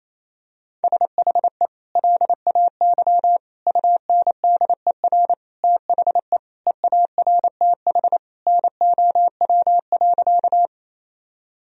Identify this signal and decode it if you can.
Morse code, heard as a keyed tone, about 28 wpm, 700 Hz. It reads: SHE LAY UNDER THE EARTH NOW.
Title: Sentences from Top 1000 Words - Review Part 2 - 28wpm